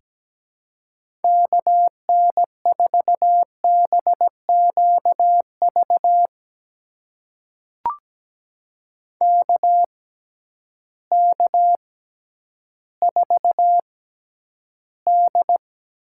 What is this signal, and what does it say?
Morse code, heard as a keyed tone, about 17 wpm, spaced as for 10 wpm, 700 Hz. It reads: KN4BQV KK4D
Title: USA Call Signs - 6x Character Spacing - 17wpm